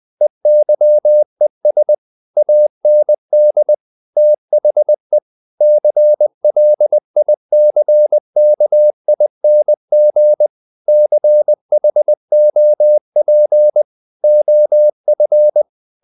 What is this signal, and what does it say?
Morse code, heard as a keyed tone, about 20 wpm, 600 Hz. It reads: EYES AND THE CLICKING CHOP OF
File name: the old man and the sea part 2 20wpm